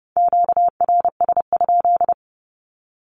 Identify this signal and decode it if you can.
Morse code, heard as a keyed tone, about 30 wpm, 700 Hz. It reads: QRS?
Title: Q-Codes 30wpm